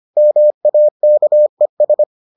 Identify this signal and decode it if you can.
Morse code, heard as a keyed tone, about 25 wpm, 600 Hz. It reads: MAKES